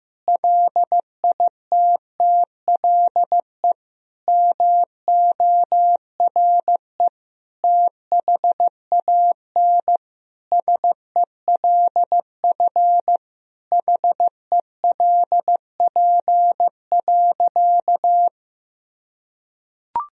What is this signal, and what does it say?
Morse code, heard as a keyed tone, about 15 wpm, 700 Hz. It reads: LITTLE MORE THAN SELF HELP.
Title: Sentences from Top 1000 Words - Review Part 1 - 15wpm